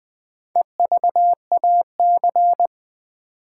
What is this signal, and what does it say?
Morse code, heard as a keyed tone, about 20 wpm, 700 Hz. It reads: EVAC
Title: Build A Word - 2 Letters At A Time - 8-Letter Words - 20wpm